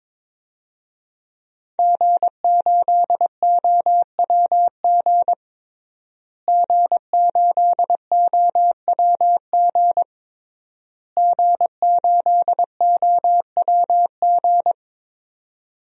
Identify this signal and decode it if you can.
Morse code, heard as a keyed tone, about 22 wpm, 700 Hz. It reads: G8OWG G8OWG G8OWG
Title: DXCC-100 Call Signs - 5 Characters - Repeated 3x - 22wpm